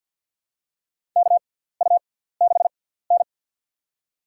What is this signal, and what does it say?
Morse code, heard as a keyed tone, about 50 wpm, spaced as for 17 wpm, 700 Hz. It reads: KU6N
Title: USA Call Signs - 4 Characters - 6x Character Spacing - 50wpm